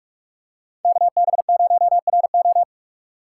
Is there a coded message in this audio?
Yes: Morse code from 0.8 to 2.6 s, heard as a keyed tone, about 40 wpm, 700 Hz. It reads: KB0RO